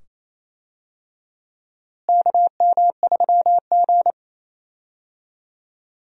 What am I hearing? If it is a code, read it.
Morse code, heard as a keyed tone, about 28 wpm, 700 Hz. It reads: KM3G
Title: USA Call Signs - 4 Characters - Repeated 2x - 28wpm